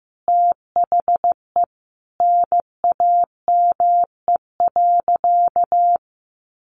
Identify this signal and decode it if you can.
Morse code, heard as a keyed tone, about 15 wpm, 700 Hz. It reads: THE NAME.
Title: Sentences from Top 1100 Words 15wpm